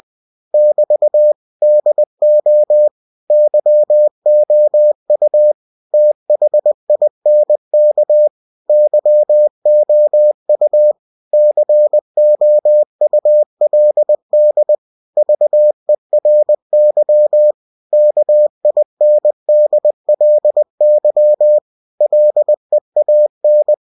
Morse code, heard as a keyed tone, about 20 wpm, 600 Hz. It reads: <BT> DO YOU THINK YOU COULD VERY KINDLY LEAN
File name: cw-books / winnie-the-pooh / winnie-20wpm-page-0076